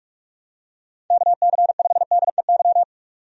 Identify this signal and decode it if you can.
Morse code, heard as a keyed tone, about 46 wpm, 700 Hz. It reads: KC5DEY
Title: USA Call Signs - 6 Characters - 45wpm